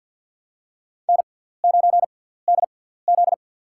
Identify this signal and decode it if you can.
Morse code, heard as a keyed tone, about 50 wpm, spaced as for 17 wpm, 700 Hz. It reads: N9DZ
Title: USA Call Signs - 6x Character Spacing - 50wpm